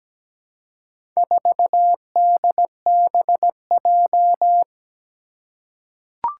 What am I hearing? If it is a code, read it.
Morse code, heard as a keyed tone, about 17 wpm, 700 Hz. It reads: 4DBJ
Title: Four Challenging Letter-Number Combinations 17wpm